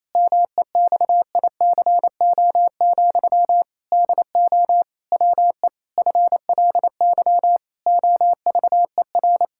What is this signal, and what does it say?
Morse code, heard as a keyed tone, about 28 wpm, 700 Hz. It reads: MEXICO, DO WE FLY OVER
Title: Pangram Challenge v2 - 28wpm